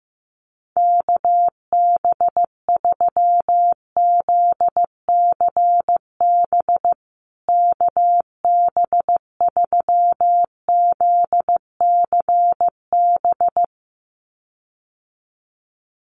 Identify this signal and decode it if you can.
Morse code, heard as a keyed tone, about 14 wpm, 700 Hz. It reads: KB3ZCB KB3ZCB
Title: USA Call Signs Repeated 2x 15wpm